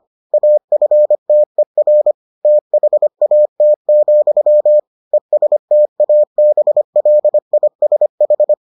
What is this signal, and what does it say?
Morse code, heard as a keyed tone, about 25 wpm, 600 Hz. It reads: AFTER THAT, ESTABLISH